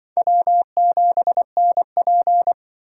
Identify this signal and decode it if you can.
Morse code, heard as a keyed tone, about 25 wpm, 700 Hz. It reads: W7NP
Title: Speed Racing - USA 1x2 Call Signs 20wpm